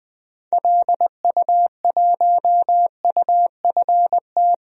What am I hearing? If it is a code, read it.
Morse code, heard as a keyed tone, about 20 wpm, 700 Hz. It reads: LU1UFT
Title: DXCC-100 Call Signs - 6 Characters - Repeated 2x - 20wpm